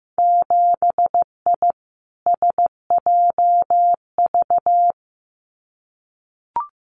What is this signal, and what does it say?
Morse code, heard as a keyed tone, about 15 wpm, 700 Hz. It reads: 7I SJV
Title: Winter Field Day Exchanges Encore 15wpm